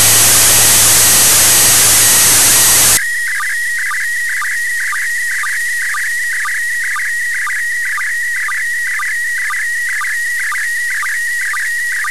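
SSTV (slow-scan television), PD 120.
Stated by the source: SSTV images sent from the ISS.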